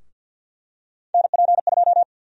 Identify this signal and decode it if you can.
Morse code, heard as a keyed tone, about 49 wpm, 700 Hz. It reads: NP2
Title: USA Call Sign Prefixes - 50wpm